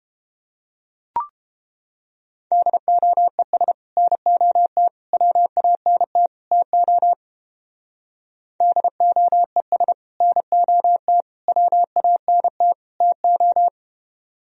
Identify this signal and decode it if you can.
Morse code, heard as a keyed tone, about 30 wpm, 700 Hz. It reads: DOES NOT WANT TO DOES NOT WANT TO